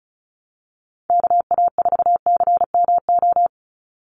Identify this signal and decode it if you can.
Morse code, heard as a keyed tone, about 36 wpm, 700 Hz. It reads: KA4CMO
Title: USA Call Signs 35wpm